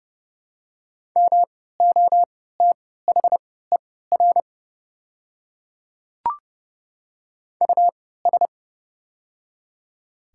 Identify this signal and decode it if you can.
Morse code, heard as a keyed tone, about 30 wpm, spaced as for 17 wpm, 700 Hz. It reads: MOTHER US
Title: 3x Character Spacing - Top 200 Words - 30wpm